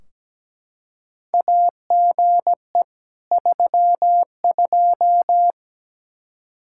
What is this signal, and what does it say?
Morse code, heard as a keyed tone, about 17 wpm, 700 Hz. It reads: AGE 32
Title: K1USN SST QSO Practice 17wpm